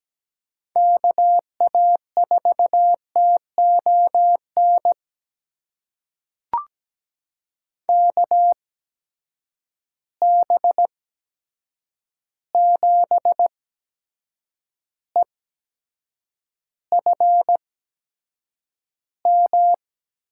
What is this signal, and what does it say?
Morse code, heard as a keyed tone, about 17 wpm, spaced as for 7 wpm, 700 Hz. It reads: KA4TON KB7EFM
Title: USA Call Signs - 6 Characters - 8x Character Spacing - 17wpm